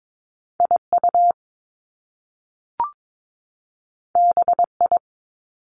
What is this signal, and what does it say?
Morse code, heard as a keyed tone, about 22 wpm, 700 Hz. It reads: IU BI
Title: Top Two Letter Combinations in Words 22wpm